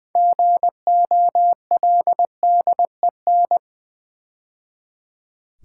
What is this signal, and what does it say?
Morse code, heard as a keyed tone, about 20 wpm, 700 Hz. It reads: GOLDEN
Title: 6 Letter Words - First 3 Letters Given - 20wpm